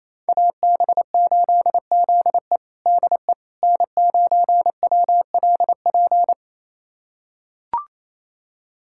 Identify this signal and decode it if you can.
Morse code, heard as a keyed tone, about 28 wpm, 700 Hz. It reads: AB8ZE DE N9WLP
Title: USA Callsign DE Callsign - 5 Characters - 28wpm